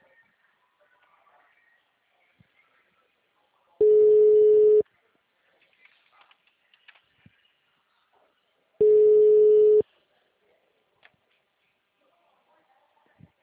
Ringback tone (Europe, 425 Hz).
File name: out- -2002- - - .2